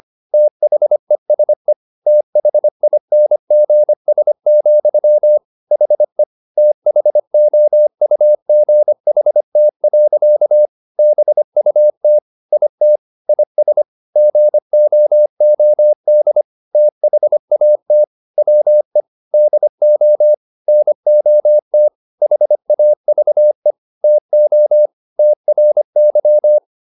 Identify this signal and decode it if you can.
Morse code, heard as a keyed tone, about 25 wpm, 600 Hz. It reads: THESE THINGS, HE THOUGHT. BUT IT IS GOOD THAT WE DO NOT HAVE TO TRY